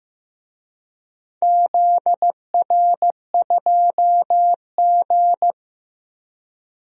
Morse code, heard as a keyed tone, about 15 wpm, 700 Hz. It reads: ZR2G
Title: DXCC-100 Call Signs - 4 Characters - Repeated 3x - 15wpm